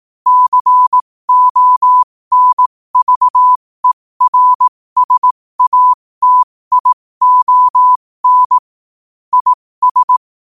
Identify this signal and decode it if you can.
Morse code, heard as a keyed tone, about 18 wpm, spaced as for 15 wpm, 1000 Hz. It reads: CONVERSATION IS
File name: qotdfiles / 15wpm